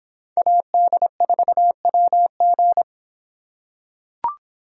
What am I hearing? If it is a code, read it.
Morse code, heard as a keyed tone, about 25 wpm, 700 Hz. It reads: AD4WG